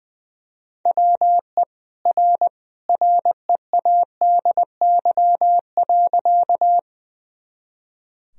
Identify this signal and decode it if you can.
Morse code, heard as a keyed tone, about 20 wpm, 700 Hz. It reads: WE R READY.
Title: Sentences from Top 400 Words and Abbreviations 20wpm